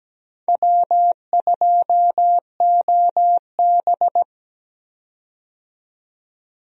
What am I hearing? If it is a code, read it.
Morse code, heard as a keyed tone, about 17 wpm, 700 Hz. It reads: W2OB
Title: USA Call Signs - 4 Characters - 17wpm